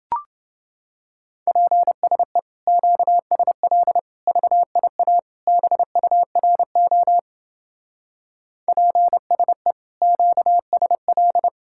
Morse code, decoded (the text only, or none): PSE QSL VIA BURO PSE QSL